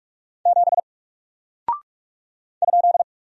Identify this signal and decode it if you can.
Morse code, heard as a keyed tone, about 44 wpm, 700 Hz. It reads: 7 ?
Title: Single Character (Letter:Number:Punctuation:Prosign) 45wpm